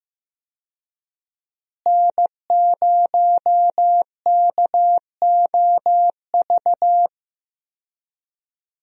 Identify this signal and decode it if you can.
Morse code, heard as a keyed tone, about 15 wpm, 700 Hz. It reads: N0KOV
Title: USA Call Signs - 5 Characters - Repeated 3x - 15wpm